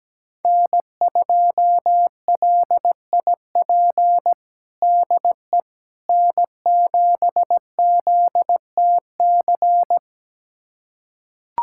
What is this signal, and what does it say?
Morse code, heard as a keyed tone, about 17 wpm, 700 Hz. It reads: N2LIP DE N7ZTC
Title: USA Callsign DE Callsign - 5 Characters - 17wpm